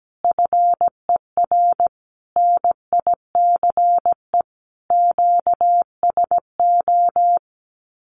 Morse code, heard as a keyed tone, about 17 wpm, 700 Hz. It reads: FER NICE QSO
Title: Common QSO Phrases 17wpm